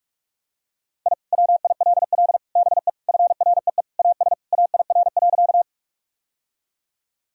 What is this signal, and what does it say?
Morse code, heard as a keyed tone, about 43 wpm, 700 Hz. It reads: I WILL BE FREE AS AIR.
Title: Sentences from Top 1000 Words - Review Part 1 - 45wpm